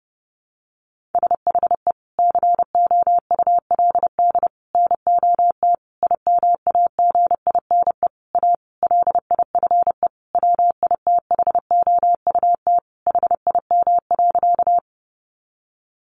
Morse code, heard as a keyed tone, about 30 wpm, 700 Hz. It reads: SHE COULD NOT IMAGINE A LIFE WITHOUT HIM.
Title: Sentences from Top 1400 Words 30wpm